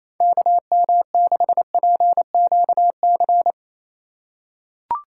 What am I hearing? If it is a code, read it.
Morse code, heard as a keyed tone, about 27 wpm, 700 Hz. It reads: KM6PQC
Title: USA Call Signs - 6 Characters - 4x Character Spacing - 28wpm